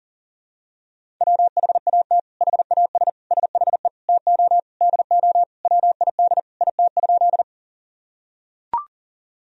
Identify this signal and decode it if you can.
Morse code, heard as a keyed tone, about 40 wpm, 700 Hz. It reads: WHAT HAS SHE TO DO WID IT?
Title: Sentences from Top 100 Words and Abbreviations 40wpm